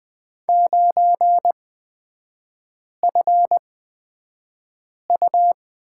Morse code, heard as a keyed tone, about 20 wpm, spaced as for 5 wpm, 700 Hz. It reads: 9FU
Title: 236 Lesson 38e - Letter q - Three Characters 20wpm